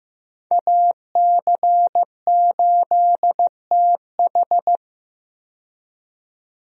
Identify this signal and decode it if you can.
Morse code, heard as a keyed tone, about 15 wpm, 700 Hz. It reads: AC8TH
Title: USA Call Signs - 5 Characters - 15wpm